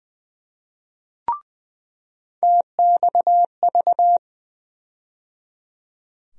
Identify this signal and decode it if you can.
Morse code, heard as a keyed tone, about 20 wpm, 700 Hz. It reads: TXV
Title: USA Call Sign Postfixes - 3 Letters 20wpm